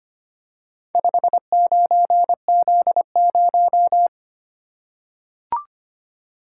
Morse code, heard as a keyed tone, about 25 wpm, 700 Hz. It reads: 59Z0
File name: practice / 25wpm / Four Letter-Number Combinations - 25wpm